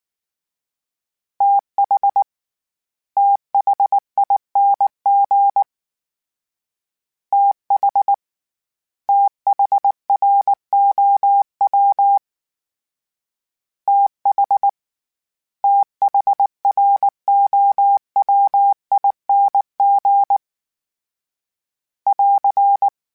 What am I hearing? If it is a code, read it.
Morse code, heard as a keyed tone, about 19 wpm, 800 Hz. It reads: TH THING TH THROW TH THROWING <AR>
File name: THs / TH / TH Prefixes 1 @18